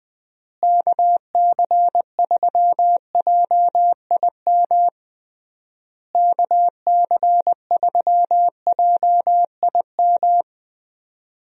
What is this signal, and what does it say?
Morse code, heard as a keyed tone, about 20 wpm, 700 Hz. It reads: KC3JIM KC3JIM
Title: USA Call Signs - 6 Characters - Repeated 3x - 20wpm